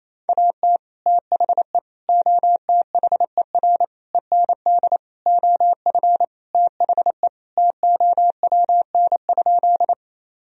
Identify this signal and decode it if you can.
Morse code, heard as a keyed tone, about 28 wpm, 700 Hz. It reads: AT THE OTHER END OF THE TOWN?
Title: Sentences from Top 1000 Words - Review Part 2 - 28wpm